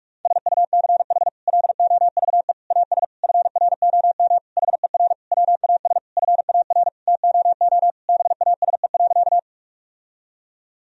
Morse code, heard as a keyed tone, about 46 wpm, 700 Hz. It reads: SUCH LOVE AS FROM HER WAS FAR TOO BASE.
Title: Sentences from Top 1000 Words - Review Part 1 - 45wpm